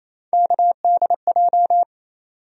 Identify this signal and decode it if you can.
Morse code, heard as a keyed tone, about 28 wpm, 700 Hz. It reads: KDJ
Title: USA Call Sign Postfixes - 3 Letters 28wpm